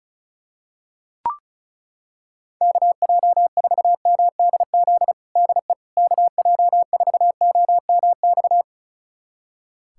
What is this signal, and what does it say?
Morse code, heard as a keyed tone, about 36 wpm, 700 Hz. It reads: KJ4MDZ DE KJ4OMX
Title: USA Callsign DE Callsign - 6 Characters - 35wpm